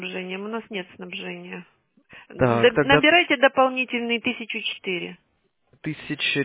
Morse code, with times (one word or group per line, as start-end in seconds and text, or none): none